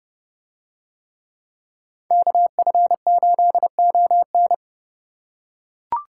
Morse code, heard as a keyed tone, about 30 wpm, 700 Hz. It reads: KF8ON